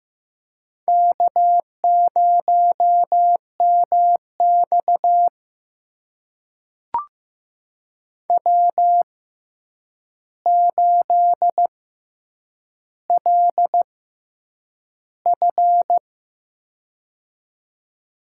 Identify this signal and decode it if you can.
Morse code, heard as a keyed tone, about 15 wpm, spaced as for 8 wpm, 700 Hz. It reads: K0MX W8LF
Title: USA Call Signs - 4 Characters - 6x Character Spacing - 15wpm